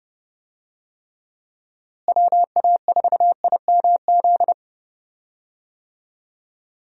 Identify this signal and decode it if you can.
Morse code, heard as a keyed tone, about 30 wpm, 700 Hz. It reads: WA4IMZ
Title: USA Call Signs - 6 Characters - Repeated 3x - 30wpm